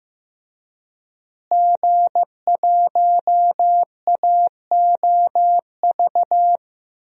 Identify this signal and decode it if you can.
Morse code, heard as a keyed tone, about 15 wpm, 700 Hz. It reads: G1AOV
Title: DXCC-100 Call Signs - 5 Characters - Repeated 3x - 15wpm